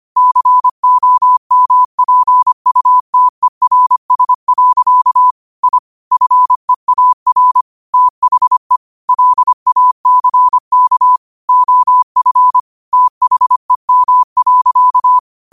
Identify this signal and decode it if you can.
Morse code, heard as a keyed tone, about 25 wpm, 1000 Hz. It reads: COMPUTERS. I FEAR THE LACK OF THEM.
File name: qotdfiles / 25wpm